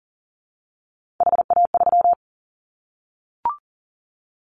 Morse code, heard as a keyed tone, about 40 wpm, 700 Hz. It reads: HA3
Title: Callsign Prefixes for Top 50 DXCC - 40wpm